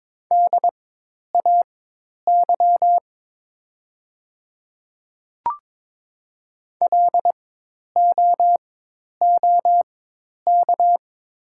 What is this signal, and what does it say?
Morse code, heard as a keyed tone, about 22 wpm, spaced as for 10 wpm, 690 Hz. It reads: DAY LOOK
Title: 4x Character Spacing - Top 100 Words - 22wpm